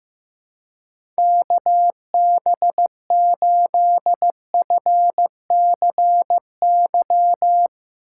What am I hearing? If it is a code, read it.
Morse code, heard as a keyed tone, about 15 wpm, 700 Hz. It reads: KB8FCY
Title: USA Call Signs - 6 Characters - 2x Character Spacing - 15wpm